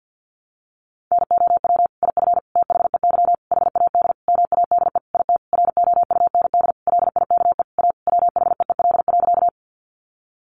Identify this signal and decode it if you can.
Morse code, heard as a keyed tone, about 50 wpm, 700 Hz. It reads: NOW IF THEY HAD MADE IT ROUND LIKE A WHEEL.
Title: Sentences from Top 500 Words 50wpm